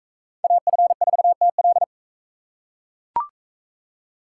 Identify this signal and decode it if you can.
Morse code, heard as a keyed tone, about 36 wpm, 700 Hz. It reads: AF4TL